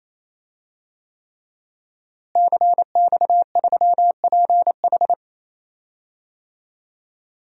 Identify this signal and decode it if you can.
Morse code, heard as a keyed tone, about 28 wpm, 700 Hz. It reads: CX3PH
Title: DXCC-100 Call Signs - 5 Characters - Repeated 3x - 28wpm